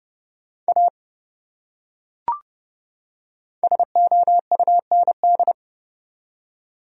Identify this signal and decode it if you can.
Morse code, heard as a keyed tone, about 30 wpm, 700 Hz. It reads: A SOUND
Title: Top 100 Words 30wpm